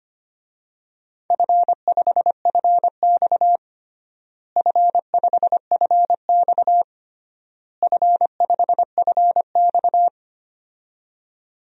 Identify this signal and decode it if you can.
Morse code, heard as a keyed tone, about 25 wpm, 700 Hz. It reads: F5FX F5FX F5FX
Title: DXCC-100 Call Signs - 4 Characters - Repeated 3x - 25wpm